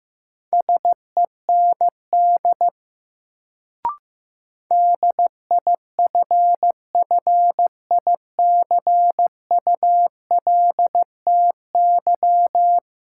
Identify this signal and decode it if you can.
Morse code, heard as a keyed tone, about 15 wpm, 700 Hz. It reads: SEND DIFFICULTY
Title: Sentences from Top 1500 Words - New Word Drill - 15wpm